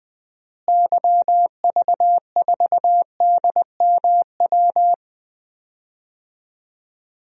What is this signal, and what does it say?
Morse code, heard as a keyed tone, about 21 wpm, 700 Hz. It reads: YV4DMW
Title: DXCC-100 Call Signs - 6 Characters - Repeated 2x - 20wpm